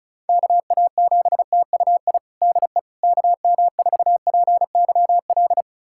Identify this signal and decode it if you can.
Morse code, heard as a keyed tone, about 36 wpm, 700 Hz. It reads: KA7TUI DE KM4PYL
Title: USA Callsign DE Callsign 35wpm